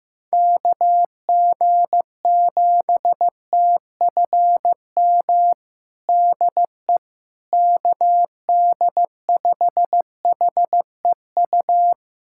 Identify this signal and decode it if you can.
Morse code, heard as a keyed tone, about 16 wpm, 700 Hz. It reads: KG7TFM DE KD5HEU
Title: USA Callsign DE Callsign - 6 Characters - 15wpm